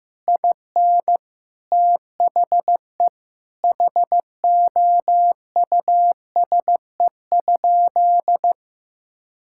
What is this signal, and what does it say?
Morse code, heard as a keyed tone, about 16 wpm, 700 Hz. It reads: IN THE HOUSE?
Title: Sentences from Top 1000 Words - Review Part 5 - 15wpm